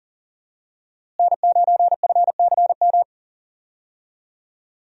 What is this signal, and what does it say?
Morse code, heard as a keyed tone, about 37 wpm, 700 Hz. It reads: N9FCM
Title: USA Call Signs - 5 Characters - Repeated 2x - 40wpm